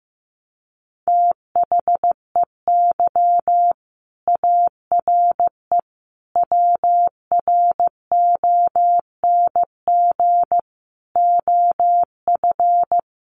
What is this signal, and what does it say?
Morse code, heard as a keyed tone, about 15 wpm, 700 Hz. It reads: THEY ARE WRONG OF
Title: Sentences from Top 900 Words 15wpm